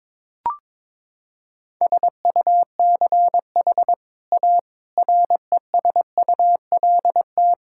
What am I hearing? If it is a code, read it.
Morse code, heard as a keyed tone, about 21 wpm, 700 Hz. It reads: SUCH A RESULT
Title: Sentences from Top 1000 Words - Review Part 5 - 22wpm